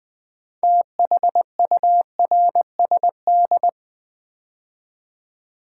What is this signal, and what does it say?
Morse code, heard as a keyed tone, about 20 wpm, 700 Hz. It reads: THURSD
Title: Build A Word - 3 Letters At A Time - 8-Letter Words - 20wpm